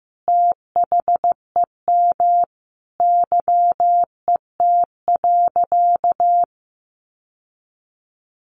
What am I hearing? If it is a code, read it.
Morse code, heard as a keyed tone, about 15 wpm, 700 Hz. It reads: THEM YET.
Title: Sentences from Top 1100 Words 15wpm